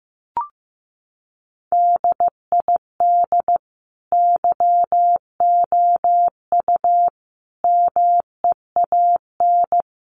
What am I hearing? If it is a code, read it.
Morse code, heard as a keyed tone, about 15 wpm, 700 Hz. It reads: DID YOU MEAN
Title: Sentences from Top 1300 Words 15wpm